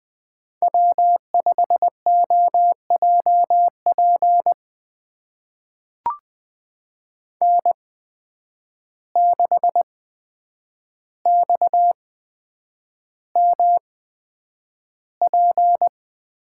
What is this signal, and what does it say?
Morse code, heard as a keyed tone, about 20 wpm, spaced as for 9 wpm, 700 Hz. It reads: W5OJP N6XMP
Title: USA Call Signs - 5 Characters - 8x Character Spacing - 20wpm